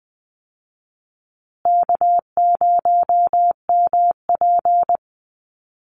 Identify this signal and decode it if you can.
Morse code, heard as a keyed tone, about 20 wpm, 700 Hz. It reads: K0MP
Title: Easy Call Signs Repeated 3x - 20wpm